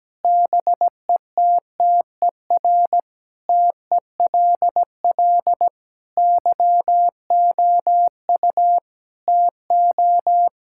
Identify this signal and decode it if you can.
Morse code, heard as a keyed tone, about 17 wpm, 700 Hz. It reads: BETTER TELL YOU TO
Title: Sentences from Top 1000 Words - Review Part 6 - 17wpm